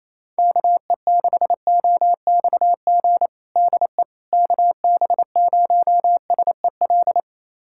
Morse code, heard as a keyed tone, about 28 wpm, 700 Hz. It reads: KE6OXG DE KB0SEL
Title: USA Callsign DE Callsign - 6 Characters - 28wpm